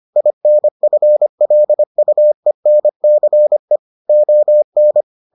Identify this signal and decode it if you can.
Morse code, heard as a keyed tone, about 25 wpm, 600 Hz. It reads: INFLUENCE ON